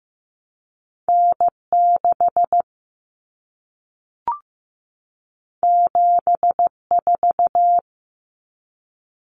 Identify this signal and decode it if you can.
Morse code, heard as a keyed tone, about 15 wpm, 700 Hz. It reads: N 6 7 4
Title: Common Cut Numbers - 2-Digit 15wpm